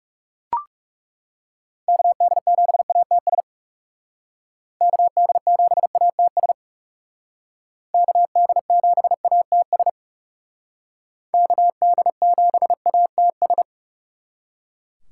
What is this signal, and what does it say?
Morse code, heard as a keyed tone, about 37 wpm, 700 Hz. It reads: KD7ATS KD7ATS KD7ATS KD7ATS
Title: Speed Racing - USA 2x3 Call Signs 30wpm